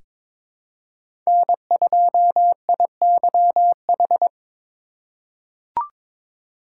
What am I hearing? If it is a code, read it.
Morse code, heard as a keyed tone, about 22 wpm, 700 Hz. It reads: N2IYH
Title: USA Call Signs - 5 Characters - 3x Character Spacing - 22wpm